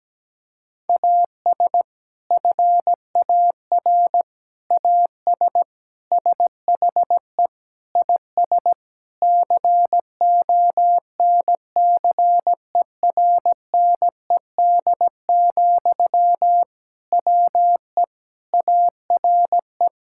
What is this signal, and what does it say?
Morse code, heard as a keyed tone, about 17 wpm, 700 Hz. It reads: AS FAR AS SHE IS CONCERNED, WE ARE
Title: Sentences from Top 1000 Words - Review Part 6 - 17wpm